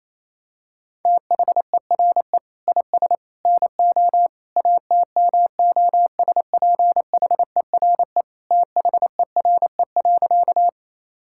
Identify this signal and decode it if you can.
Morse code, heard as a keyed tone, about 28 wpm, 700 Hz. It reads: THERE IS NO ATMOSPHERE THERE.
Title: Sentences from Top 1600 Words 28wpm